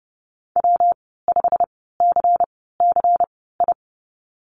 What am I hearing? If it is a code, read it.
Morse code, heard as a keyed tone, about 30 wpm, spaced as for 17 wpm, 700 Hz. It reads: W5CCI
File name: Lesson 19g - Letter w - Call Signs 30wpm